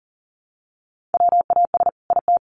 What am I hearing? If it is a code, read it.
Morse code, heard as a keyed tone, about 40 wpm, 700 Hz. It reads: WAS IT
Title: Sentences from Top 1400 Words 40wpm